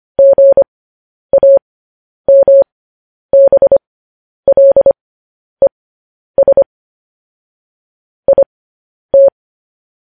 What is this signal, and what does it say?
Morse code, heard as a keyed tone, about 25 wpm, spaced as for 10 wpm, 550 Hz. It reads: GAMBLES IT
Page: News Headlines in Morse Code at 10 WPM